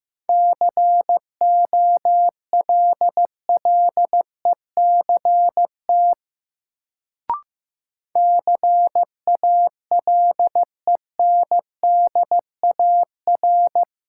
Morse code, heard as a keyed tone, about 15 wpm, 700 Hz. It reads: COLLECT CALENDAR
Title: Sentences from Top 1600 Words - New Word Drill Alt - 15wpm